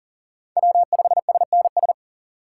Morse code, heard as a keyed tone, about 40 wpm, 700 Hz. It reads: W5SNS
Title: USA Call Signs - 5 Characters - 6x Character Spacing - 40wpm